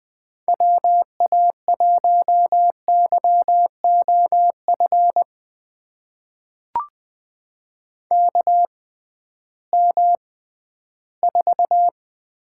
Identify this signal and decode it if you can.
Morse code, heard as a keyed tone, about 20 wpm, spaced as for 13 wpm, 700 Hz. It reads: WA1YOF KM4
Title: USA Call Signs - 6 Characters - 6x Character Spacing - 20wpm